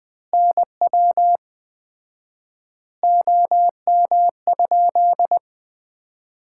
Morse code, heard as a keyed tone, about 20 wpm, 700 Hz. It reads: NW OM?
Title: Common QSO Phrases - 4x Word-Spacing - 20wpm